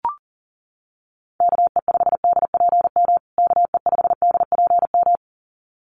Morse code, heard as a keyed tone, about 40 wpm, 700 Hz. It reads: KE5DPM KE5DPM
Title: USA Call Signs Repeated 2x 40wpm